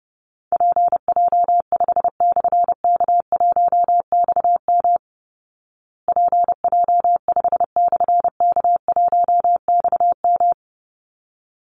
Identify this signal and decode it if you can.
Morse code, heard as a keyed tone, about 30 wpm, 700 Hz. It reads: PJ5/K1XM PJ5/K1XM
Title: Call Signs Repeated 3x - 30wpm